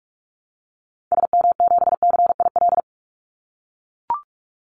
Morse code, heard as a keyed tone, about 44 wpm, 700 Hz. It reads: SM7CIL
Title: Call Signs - 45wpm